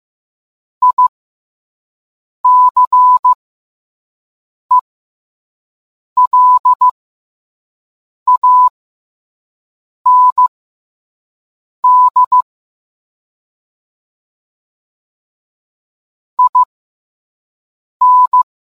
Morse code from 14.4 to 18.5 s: IN